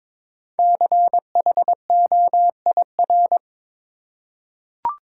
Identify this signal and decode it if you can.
Morse code, heard as a keyed tone, about 22 wpm, 700 Hz. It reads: CHOIR
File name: 5 Letter Words - First 3 Letters Given - 22wpm